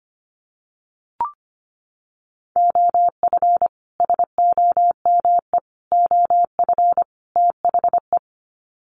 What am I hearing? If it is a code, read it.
Morse code, heard as a keyed tone, about 25 wpm, 700 Hz. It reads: OF SOME OF THE
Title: Sets of 4 Words 25wpm